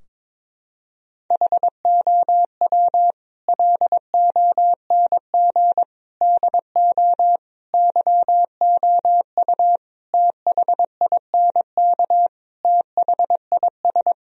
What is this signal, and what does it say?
Morse code, heard as a keyed tone, about 22 wpm, 700 Hz. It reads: HOW LONG DO YOU THINK THIS
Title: Sentences from Top 1000 Words - Review Part 2 - 22wpm